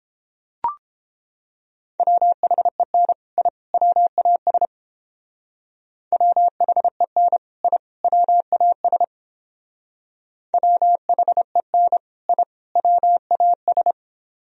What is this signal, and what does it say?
Morse code, heard as a keyed tone, about 29 wpm, 700 Hz. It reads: WHEN I WAS WHEN I WAS WHEN I WAS